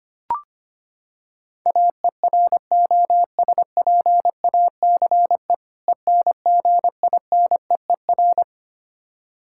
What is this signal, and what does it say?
Morse code, heard as a keyed tone, about 25 wpm, 700 Hz. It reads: AEROSPACE ENGINEER
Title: Practice QSO - Exchange 3 - More Info - 25wpm